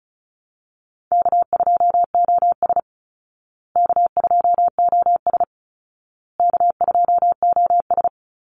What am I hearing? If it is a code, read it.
Morse code, heard as a keyed tone, about 34 wpm, 700 Hz. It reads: K2OS K2OS K2OS
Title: Call Signs Repeated 2x - 35wpm